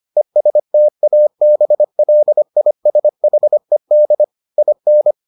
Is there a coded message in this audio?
Yes: Morse code from 0.2 to 5.1 s, heard as a keyed tone, about 25 wpm, 600 Hz. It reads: ESTABLISHED IN